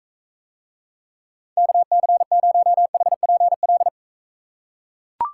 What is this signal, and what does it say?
Morse code, heard as a keyed tone, about 43 wpm, 700 Hz. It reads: KC0HPL